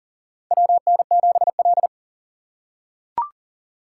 Morse code, heard as a keyed tone, about 40 wpm, 700 Hz. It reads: WN7L
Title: USA Call Signs - 4 Characters - 3x Character Spacing - 40wpm